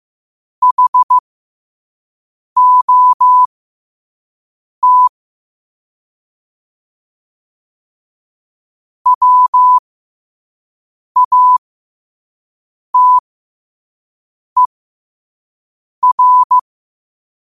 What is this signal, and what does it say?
Morse code, heard as a keyed tone, about 15 wpm, spaced as for 5 wpm, 1000 Hz. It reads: HOT WATER
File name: qotdfiles / 5wpm